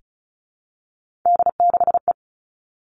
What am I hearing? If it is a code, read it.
Morse code, heard as a keyed tone, about 36 wpm, 700 Hz. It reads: D6E
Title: Three Challenging Letter-Number Combinations 35wpm